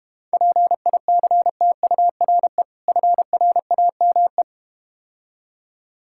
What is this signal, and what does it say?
Morse code, heard as a keyed tone, about 32 wpm, 690 Hz. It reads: PICTURE FRAME